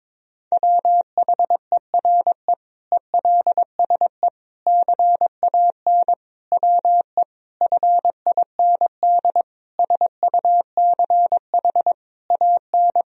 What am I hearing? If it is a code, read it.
Morse code, heard as a keyed tone, about 22 wpm, 700 Hz. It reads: WHERE ELSE CAN WE FIND SUCH AN
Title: Sentences from Top 1000 Words - Review Part 1 - 22wpm